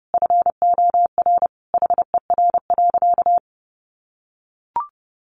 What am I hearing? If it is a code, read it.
Morse code, heard as a keyed tone, about 30 wpm, 700 Hz. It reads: FOR HER.
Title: Sentences from Top 600 Words 30wpm